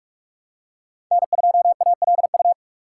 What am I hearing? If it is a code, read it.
Morse code, heard as a keyed tone, about 46 wpm, 700 Hz. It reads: N2ALU
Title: USA Call Signs - 5 Characters - 4x Character Spacing - 45wpm